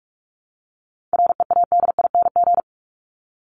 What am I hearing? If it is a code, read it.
Morse code, heard as a keyed tone, about 44 wpm, 700 Hz. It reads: READING
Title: Top 501-750 Words 45wpm